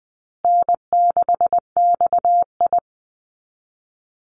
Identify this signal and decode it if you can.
Morse code, heard as a keyed tone, about 20 wpm, 700 Hz. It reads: N6XI
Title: Easy Call Signs Repeated 3x - 20wpm